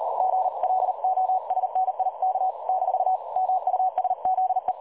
Morse code, heard as a keyed tone, about 37 wpm, 750 Hz. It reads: GLOSNIK - MUSZE
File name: dlazaawansowanych / nagraniazpasma / websdr recording 3565.0kHz